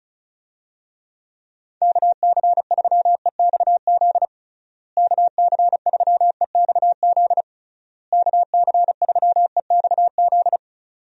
Morse code, heard as a keyed tone, about 34 wpm, 700 Hz. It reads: KC3EXZ KC3EXZ KC3EXZ
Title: USA Call Signs - 6 Characters - Repeated 3x - 35wpm